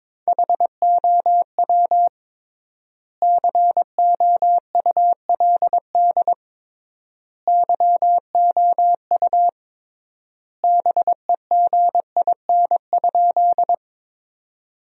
Morse code, heard as a keyed tone, about 21 wpm, 700 Hz. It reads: HOW COULD YOU BEGIN?
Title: 3x Word-Spacing - Sentences from Top 300 Words - 22wpm